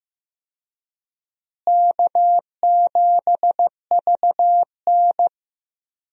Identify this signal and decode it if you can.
Morse code, heard as a keyed tone, about 15 wpm, 700 Hz. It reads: K7VN
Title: USA Call Signs - 4 Characters - Repeated 3x - 15wpm